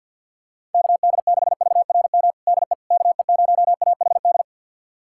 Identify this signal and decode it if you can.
Morse code, heard as a keyed tone, about 52 wpm, 700 Hz. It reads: KD6VRM DE KE0AHD
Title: USA Callsign DE Callsign 50wpm